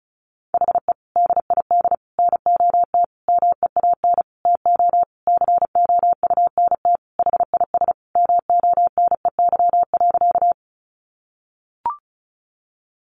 Morse code, heard as a keyed tone, about 34 wpm, 700 Hz. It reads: HE DID NOT MEAN TO COUNT HIS MONEY.
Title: Sentences from Top 1500 Words 35wpm